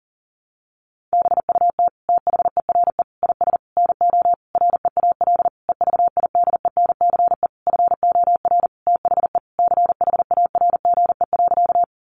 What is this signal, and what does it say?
Morse code, heard as a keyed tone, about 40 wpm, 700 Hz. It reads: BUT THERE IS NO REAL EVIDENCE FOR THE CHARGE.
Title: Sentences from Top 1300 Words 40wpm